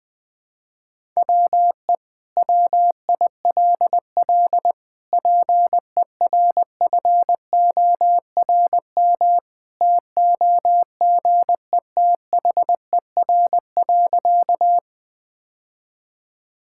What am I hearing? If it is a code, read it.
Morse code, heard as a keyed tone, about 20 wpm, 700 Hz. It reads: WE WILL PERFORM TOGETHER.
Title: Sentences from Top 1000 Words - Review Part 5 - 20wpm